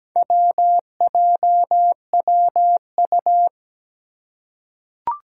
Morse code, heard as a keyed tone, about 17 wpm, 700 Hz. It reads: WJWU
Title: Four Challenging Letter-Number Combinations 17wpm